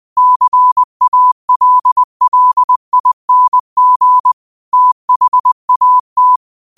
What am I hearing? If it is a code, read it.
Morse code, heard as a keyed tone, about 20 wpm, 1000 Hz. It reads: CALLING THAT
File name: qotdfiles / 20wpm